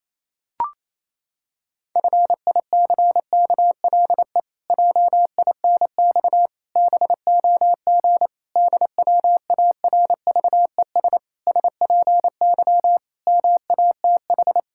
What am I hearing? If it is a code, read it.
Morse code, heard as a keyed tone, about 28 wpm, 700 Hz. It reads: FICKLE JINX BOG DWARVES SPY MATH
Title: Pangram Challenge v2 - 28wpm